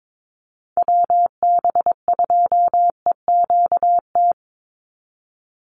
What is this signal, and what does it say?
Morse code, heard as a keyed tone, about 22 wpm, 700 Hz. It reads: WB2EQT
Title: USA Call Signs 22wpm